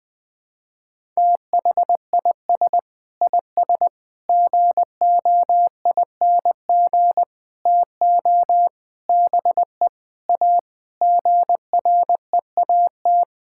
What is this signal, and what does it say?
Morse code, heard as a keyed tone, about 20 wpm, 700 Hz. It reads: THIS IS GOING TO BE A GREAT
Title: Sentences with 6-Word Phrases - 20wpm